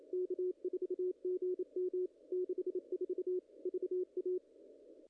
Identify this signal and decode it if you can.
Morse code, heard as a keyed tone, about 28 wpm, 360 Hz. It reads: K4GM 64 VA
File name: contests / ss-cw-k5tr-op-n5rz -0000